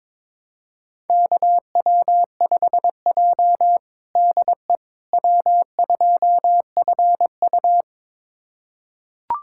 Morse code, heard as a keyed tone, about 22 wpm, 700 Hz. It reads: KW5J DE W2FU
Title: USA Callsign DE Callsign - 4 Characters - 22wpm